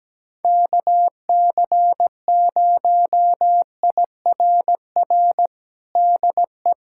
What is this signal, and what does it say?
Morse code, heard as a keyed tone, about 17 wpm, 700 Hz. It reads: KC0IRR DE
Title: USA Callsign DE Callsign 17wpm